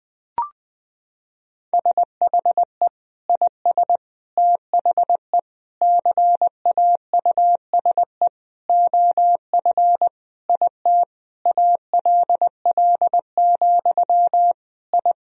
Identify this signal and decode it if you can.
Morse code, heard as a keyed tone, about 20 wpm, 700 Hz. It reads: SHE IS THE CAUSE OF IT ALL, I
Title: Sentences from Top 1000 Words - Review Part 1 - 20wpm